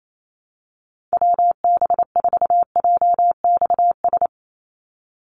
Morse code, heard as a keyed tone, about 28 wpm, 700 Hz. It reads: WB4JXS
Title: USA Call Signs 28wpm